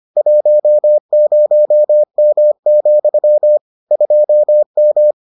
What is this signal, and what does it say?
Morse code, heard as a keyed tone, about 25 wpm, 600 Hz. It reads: 10M, 2M